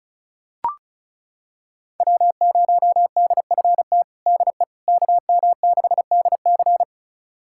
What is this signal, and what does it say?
Morse code, heard as a keyed tone, about 34 wpm, 700 Hz. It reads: W0DFT DE KM6DC
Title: USA Callsign DE Callsign - 5 Characters - 35wpm